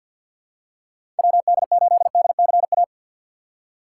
Morse code, heard as a keyed tone, about 49 wpm, 700 Hz. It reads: WD8DCA
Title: USA Call Signs - 3x Character Spacing - 50wpm